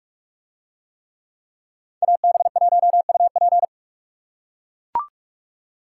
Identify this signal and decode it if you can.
Morse code, heard as a keyed tone, about 43 wpm, 700 Hz. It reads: AB1UP